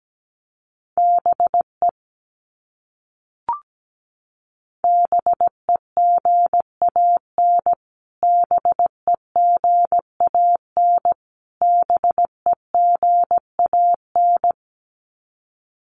Morse code, heard as a keyed tone, about 17 wpm, 700 Hz. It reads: BE BEGAN BEGAN BEGAN
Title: Top 500 Words Repeated 17wpm